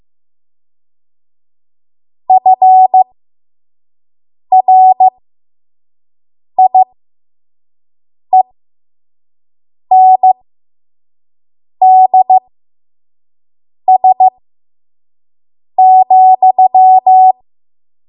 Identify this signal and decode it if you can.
Morse code, heard as a keyed tone, about 15 wpm, spaced as for 5 wpm, 750 Hz. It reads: FRIENDS,
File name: Morse / 05WPM